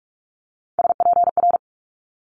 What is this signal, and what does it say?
Morse code, heard as a keyed tone, about 46 wpm, 700 Hz. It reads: SPR
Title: Multi-letter Phonemes Quick Fire 45wpm